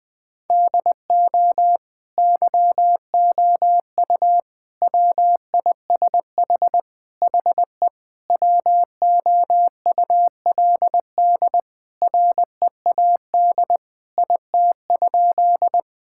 Morse code, heard as a keyed tone, about 20 wpm, 700 Hz. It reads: DO YOU WISH HE WOULD READ IT?
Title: Sentences from Top 1600 Words 20wpm